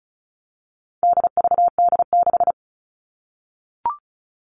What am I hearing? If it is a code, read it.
Morse code, heard as a keyed tone, about 34 wpm, 700 Hz. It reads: DVD6
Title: All Cut Numbers - 4-Digit 35wpm